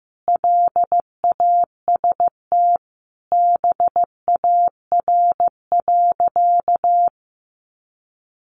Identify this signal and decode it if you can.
Morse code, heard as a keyed tone, about 14 wpm, 700 Hz. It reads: LAST BAR.
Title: Easy Phrases 15wpm